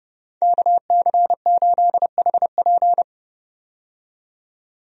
Morse code, heard as a keyed tone, about 30 wpm, 700 Hz. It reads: KC8HP
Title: USA Call Signs - 5 Characters - 30wpm